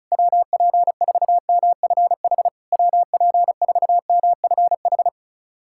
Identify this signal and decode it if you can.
Morse code, heard as a keyed tone, about 36 wpm, 700 Hz. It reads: WP4MFH WP4MFH
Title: SOTA QSO Practice 35wpm